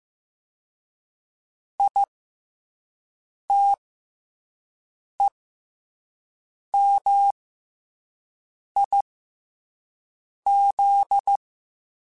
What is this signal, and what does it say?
Morse code, heard as a keyed tone, about 15 wpm, spaced as for 5 wpm, 800 Hz. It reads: ITEMIZ
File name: Special INGs-mp3s / Special INGs-mp3s / SlowToFaster ing @10